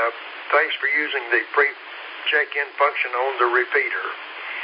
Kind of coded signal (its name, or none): Emergency Alert System (SAME) tones